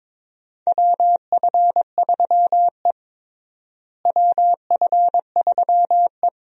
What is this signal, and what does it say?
Morse code, heard as a keyed tone, about 22 wpm, 700 Hz. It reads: WF3E WF3E
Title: USA Call Signs - 4 Characters - Repeated 2x - 22wpm